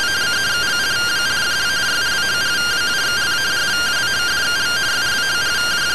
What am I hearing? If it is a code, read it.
Two-tone teleprinter signal, about 170 Hz shift.